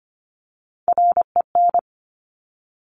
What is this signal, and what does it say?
Morse code, heard as a keyed tone, about 25 wpm, 700 Hz. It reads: REN
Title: Top Three Letter Combinations in Words 25wpm